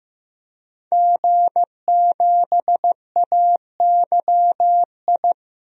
Morse code, heard as a keyed tone, about 15 wpm, 700 Hz. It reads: G7AYI
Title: DXCC-100 Call Signs - 5 Characters - 15wpm